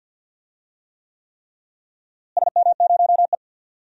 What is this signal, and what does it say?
Morse code, heard as a keyed tone, about 50 wpm, 700 Hz. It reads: SM0E
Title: DXCC-100 Call Signs - 4 Characters - Repeated 3x - 50wpm